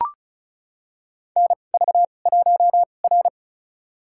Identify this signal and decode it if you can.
Morse code, heard as a keyed tone, about 36 wpm, spaced as for 25 wpm, 700 Hz. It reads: NV1R
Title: USA Call Signs - 4 Characters - 2x Character Spacing - 35wpm